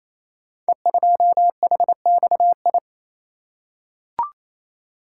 Morse code, heard as a keyed tone, about 27 wpm, 700 Hz. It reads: E2HXI
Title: Code Groups - 5 Letter:Numbers - 28wpm